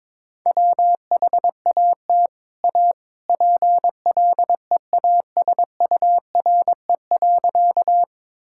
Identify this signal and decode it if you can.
Morse code, heard as a keyed tone, about 22 wpm, 700 Hz. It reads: WHAT A PLEASURE.
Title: Sentences from Top 1600 Words 22wpm